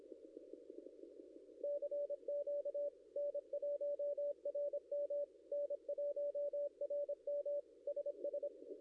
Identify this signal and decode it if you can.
Morse code, heard as a keyed tone, about 26 wpm, 580 Hz. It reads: CQ N1RM N1RM SS
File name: contests / ss-cw-k5tr-op-n5rz -0700